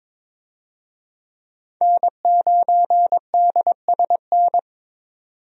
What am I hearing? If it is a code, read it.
Morse code, heard as a keyed tone, about 22 wpm, 700 Hz. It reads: N9DSN
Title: USA Call Signs - 5 Characters - 22wpm